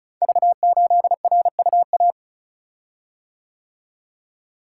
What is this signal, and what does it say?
Morse code, heard as a keyed tone, about 36 wpm, 700 Hz. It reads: V8RUA
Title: Code Groups - 5 Letter:Numbers - 35wpm